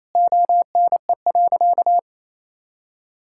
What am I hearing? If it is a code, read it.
Morse code, heard as a keyed tone, about 28 wpm, 700 Hz. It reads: ONE.
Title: Sentences from Top 1000 Words - Review Part 6 - 28wpm